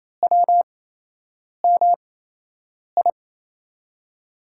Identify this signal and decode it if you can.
Morse code, heard as a keyed tone, about 28 wpm, spaced as for 8 wpm, 700 Hz. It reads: WMI
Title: USA Call Signs - 5 Characters - 8x Character Spacing - 28wpm